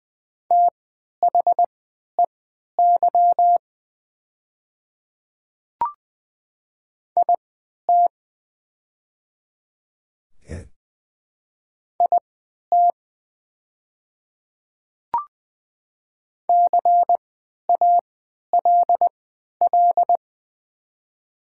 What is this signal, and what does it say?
Morse code, heard as a keyed tone, about 20 wpm, spaced as for 11 wpm, 700 Hz. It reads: THEY IT IT CALL
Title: 3x Character Spacing - Top 100 Words - 20wpm